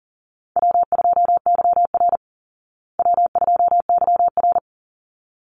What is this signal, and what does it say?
Morse code, heard as a keyed tone, about 40 wpm, 700 Hz. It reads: W2YR W2YR
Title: Call Signs Repeated 3x - 40wpm